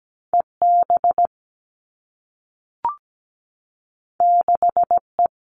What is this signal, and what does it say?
Morse code, heard as a keyed tone, about 17 wpm, 700 Hz. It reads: EB 6E
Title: All Cut Numbers - 2-Digit 17wpm